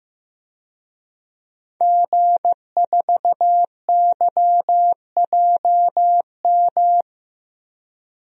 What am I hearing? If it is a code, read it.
Morse code, heard as a keyed tone, about 15 wpm, 700 Hz. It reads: G4YJM
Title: DXCC-100 Call Signs - 5 Characters - 15wpm